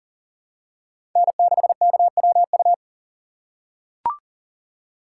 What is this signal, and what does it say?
Morse code, heard as a keyed tone, about 40 wpm, 700 Hz. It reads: N6KWU
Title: USA Call Signs - 5 Characters - 4x Character Spacing - 40wpm